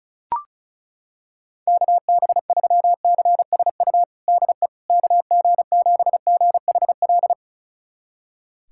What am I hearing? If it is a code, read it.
Morse code, heard as a keyed tone, about 36 wpm, 700 Hz. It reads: KB3CSU DE KG7GHL
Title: USA Callsign DE Callsign - 6 Characters - 35wpm